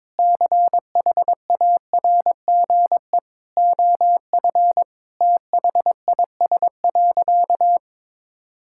Morse code, heard as a keyed tone, about 21 wpm, 700 Hz. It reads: CHARGE OF THIS.
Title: Sentences from Top 1000 Words - Review Part 6 - 22wpm